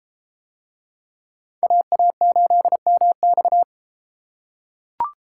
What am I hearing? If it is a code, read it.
Morse code, heard as a keyed tone, about 33 wpm, 700 Hz. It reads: AA8MX